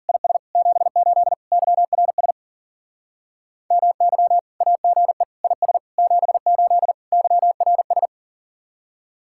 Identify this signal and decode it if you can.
Morse code, heard as a keyed tone, about 49 wpm, 700 Hz. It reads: IS 78 YRS MY AGE IS 78 YRS
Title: Speed-Racing Common QSO Phrases - 40wpm